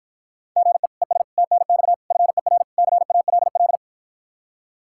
Morse code, heard as a keyed tone, about 54 wpm, 700 Hz. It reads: GE ES TNX FER CALL